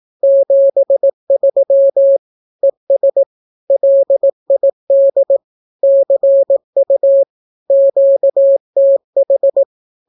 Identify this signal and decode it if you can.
Morse code, heard as a keyed tone, about 18 wpm, 550 Hz. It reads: 73 ES LID CU QTH
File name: naukapodstaw / skroty / skroty